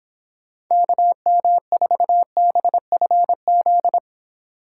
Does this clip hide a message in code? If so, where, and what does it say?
Yes: Morse code from 0.7 to 4.0 s, heard as a keyed tone, about 25 wpm, 700 Hz. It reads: KM4BFZ